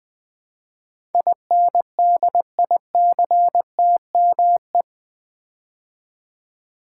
Morse code, heard as a keyed tone, about 20 wpm, 700 Hz. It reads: INDICTME
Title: Build A Word - 4 Letters At A Time - 10-Letter Words - 20wpm